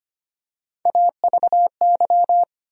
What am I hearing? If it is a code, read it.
Morse code, heard as a keyed tone, about 25 wpm, 700 Hz. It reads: AVY
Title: USA Call Sign Postfixes - 3 Letters 25wpm